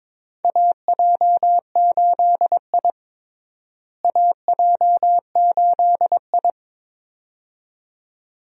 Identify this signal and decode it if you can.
Morse code, heard as a keyed tone, about 22 wpm, 700 Hz. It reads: AJ8I AJ8I
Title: USA Call Signs - 4 Characters - Repeated 2x - 22wpm